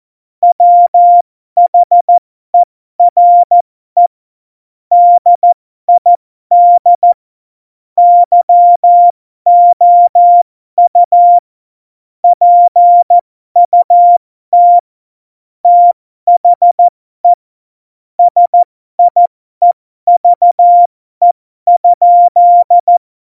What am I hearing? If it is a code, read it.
Morse code, heard as a keyed tone, about 14 wpm, spaced as for 12 wpm, 700 Hz. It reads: WHERE DID YOU PUT THE SIEVE?
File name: Morsekurs / Witze